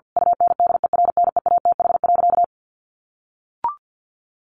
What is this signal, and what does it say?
Morse code, heard as a keyed tone, about 48 wpm, 700 Hz. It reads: UNDERNEATH.
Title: A Sound Of Thunder 50wpm